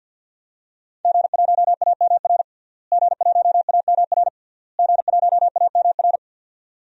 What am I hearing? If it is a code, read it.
Morse code, heard as a keyed tone, about 48 wpm, 700 Hz. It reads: G1AMR G1AMR G1AMR
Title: DXCC-100 Call Signs - 5 Characters - Repeated 3x - 50wpm